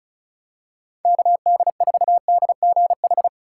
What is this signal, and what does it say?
Morse code, heard as a keyed tone, about 36 wpm, 700 Hz. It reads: KD4DGH
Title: USA Call Signs - 6 Characters - 4x Character Spacing - 35wpm